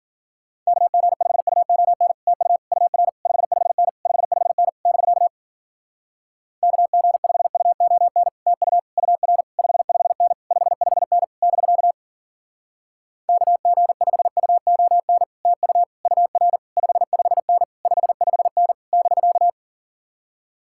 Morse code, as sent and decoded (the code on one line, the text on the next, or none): -.- --. ..... ..- --- -. / - ..- / ..- .-. / ..... ..... -. / ..... ..... -. / -...-.- / -.- --. ..... ..- --- -. / - ..- / ..- .-. / ..... ..... -. / ..... ..... -. / -...-.- / -.- --. ..... ..- --- -. / - ..- / ..- .-. / ..... ..... -. / ..... ..... -. / -...-.-
KG5UON TU UR 55N 55N <BK> KG5UON TU UR 55N 55N <BK> KG5UON TU UR 55N 55N <BK>